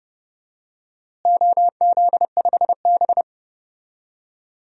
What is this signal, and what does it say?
Morse code, heard as a keyed tone, about 31 wpm, 700 Hz. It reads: OZ5B
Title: DXCC-100 Call Signs - 4 Characters - Repeated 3x - 30wpm